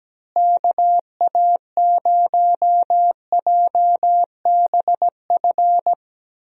Morse code, heard as a keyed tone, about 17 wpm, 700 Hz. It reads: KA0JBF
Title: USA Call Signs - 6 Characters - Repeated 3x - 17wpm